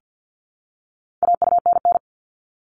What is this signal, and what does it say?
Morse code, heard as a keyed tone, about 50 wpm, 700 Hz. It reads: AUNN
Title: All Cut Numbers - 4-Digit 50wpm